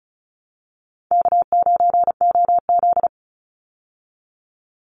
Morse code, heard as a keyed tone, about 34 wpm, 700 Hz. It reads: K9OZ
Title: Easy Call Signs Repeated 3x - 35wpm